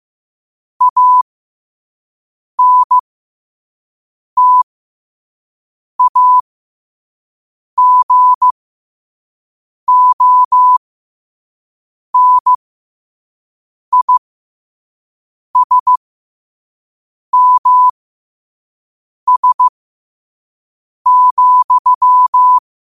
Morse code, heard as a keyed tone, about 15 wpm, spaced as for 5 wpm, 1000 Hz. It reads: ANTAGONISMS,
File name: qotdfiles / 5wpm